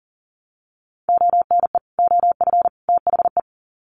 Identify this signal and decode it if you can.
Morse code, heard as a keyed tone, about 40 wpm, 700 Hz. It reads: ONE OF THE
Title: Sets of 3 Words 40wpm